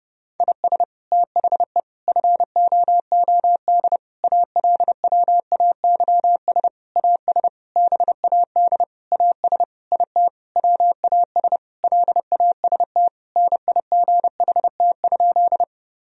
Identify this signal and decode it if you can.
Morse code, heard as a keyed tone, about 30 wpm, 700 Hz. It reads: IS THE FOOD ALWAYS AS BAD AS IT WAS LAST NIGHT?
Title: Sentences from Top 1000 Words - Review Part 1 - 30wpm